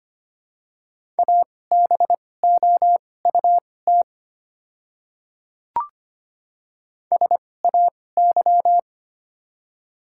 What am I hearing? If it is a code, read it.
Morse code, heard as a keyed tone, about 25 wpm, spaced as for 18 wpm, 700 Hz. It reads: ABOUT SAY
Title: 2x Character Spacing - Top 300 Words - 25wpm